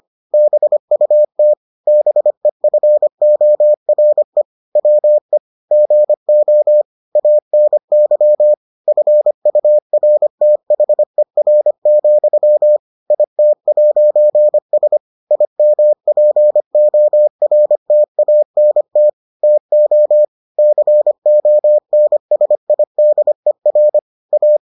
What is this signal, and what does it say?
Morse code, heard as a keyed tone, about 25 wpm, 600 Hz. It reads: BUT BEFORE WE GO ANY FURTHER, IT'S IMPORTANT TO CONSIDER A